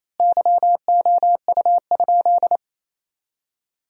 Morse code, heard as a keyed tone, about 28 wpm, 700 Hz. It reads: YOU?
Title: 4x Word-Spacing - Sentences from Top 100 Words - 28wpm